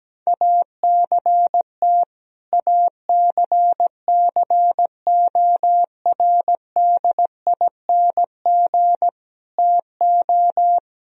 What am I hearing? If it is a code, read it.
Morse code, heard as a keyed tone, about 17 wpm, 700 Hz. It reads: ACT ACCORDING TO
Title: Sentences from Top 1000 Words - Review Part 4 - 17wpm